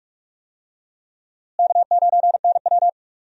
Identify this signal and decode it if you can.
Morse code, heard as a keyed tone, about 46 wpm, 700 Hz. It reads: K9NW
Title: USA Call Signs - 4 Characters - Repeated 2x - 45wpm